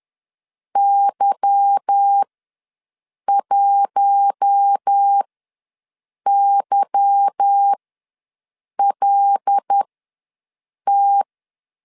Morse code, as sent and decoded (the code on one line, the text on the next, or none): -.-- .---- -.-- .-.. -
Y1YLT